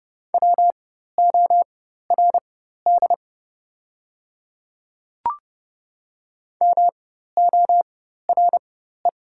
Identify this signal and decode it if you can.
Morse code, heard as a keyed tone, about 31 wpm, spaced as for 14 wpm, 700 Hz. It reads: WORD MORE
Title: 4x Character Spacing - Top 100 Words - 30wpm